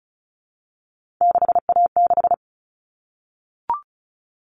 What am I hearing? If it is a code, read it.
Morse code, heard as a keyed tone, about 34 wpm, 700 Hz. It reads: 6A6
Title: All Cut Numbers - 3-Digit 35wpm